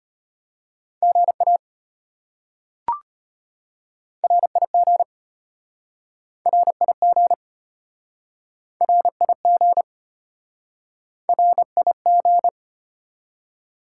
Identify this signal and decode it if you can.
Morse code, heard as a keyed tone, about 34 wpm, 700 Hz. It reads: GA RIG RIG RIG RIG